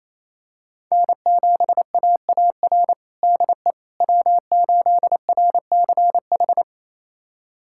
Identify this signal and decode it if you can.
Morse code, heard as a keyed tone, about 28 wpm, 700 Hz. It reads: N7AAR DE W8RCH
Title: USA Callsign DE Callsign - 5 Characters - 28wpm